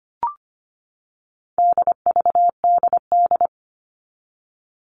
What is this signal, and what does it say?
Morse code, heard as a keyed tone, about 25 wpm, 700 Hz. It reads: DVDD
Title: All Cut Numbers - 4-Digit 25wpm